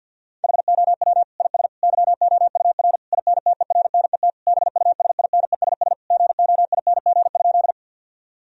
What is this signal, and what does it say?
Morse code, heard as a keyed tone, about 50 wpm, 700 Hz. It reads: HOW IS YOUR INTERNET BUSINESS GOING?
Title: Sentences from Top 1000 Words - Review Part 4 - 50wpm